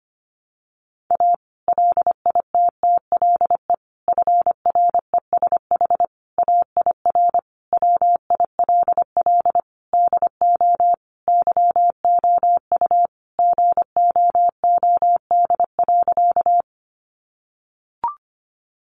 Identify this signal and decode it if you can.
Morse code, heard as a keyed tone, about 25 wpm, 700 Hz. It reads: A LITTLE FRESH AIR WILL DO YOU GOOD.
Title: Sentences from Top 1200 Words 25wpm